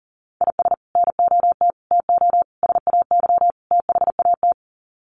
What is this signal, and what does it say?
Morse code, heard as a keyed tone, about 40 wpm, 700 Hz. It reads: IS NOT TO SAY THAT
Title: Sets of 5 Words 40wpm